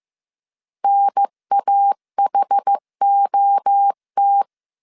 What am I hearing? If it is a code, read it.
Morse code, heard as a keyed tone, about 15 wpm, 800 Hz. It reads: NAHOT
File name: CW / DET / 1 3 1500C S A 14wpm 2447